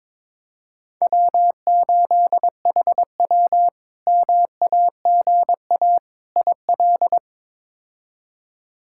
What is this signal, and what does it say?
Morse code, heard as a keyed tone, about 22 wpm, 700 Hz. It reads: W8HW MAGA IL
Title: K1USN SST QSO Practice - 22wpm